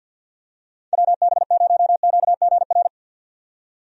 Morse code, heard as a keyed tone, about 51 wpm, 700 Hz. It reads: WB0QGR
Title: USA Call Signs - 6 Characters - 50wpm